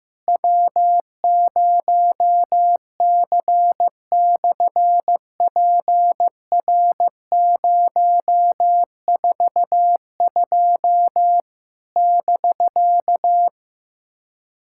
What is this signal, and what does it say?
Morse code, heard as a keyed tone, about 15 wpm, 700 Hz. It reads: W0C/PR042 <BK>
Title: SOTA QSO Practice 15wpm